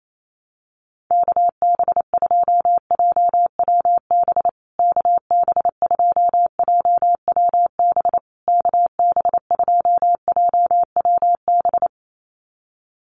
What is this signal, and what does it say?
Morse code, heard as a keyed tone, about 28 wpm, 700 Hz. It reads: KB2JWB KB2JWB KB2JWB
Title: USA Call Signs Repeated 3x 28wpm